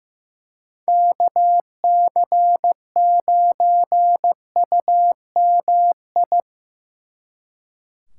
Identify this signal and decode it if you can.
Morse code, heard as a keyed tone, about 18 wpm, 700 Hz. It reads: KC9UMI
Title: Speed Racing - USA 2x3 Call Signs 15wpm